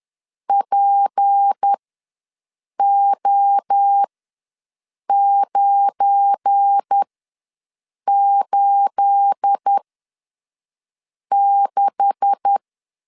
Morse code, heard as a keyed tone, about 10 wpm, spaced as for 5 wpm, 800 Hz. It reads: PO986
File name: LEZIONI / 10 SIGNS / 10A SLASH ws 10wpm 4121